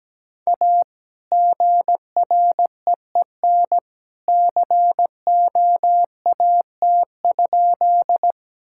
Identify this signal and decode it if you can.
Morse code, heard as a keyed tone, about 17 wpm, 700 Hz. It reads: A GREEN COAT?
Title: Sentences from Top 1600 Words 17wpm